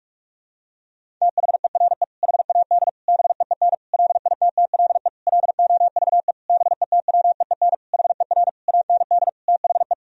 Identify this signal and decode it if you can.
Morse code, heard as a keyed tone, about 41 wpm, 700 Hz. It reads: THERE HAD BEEN LITTLE LOVE BETWEEN HER AND THE
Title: Sentences from Top 1000 Words - Review Part 1 - 45wpm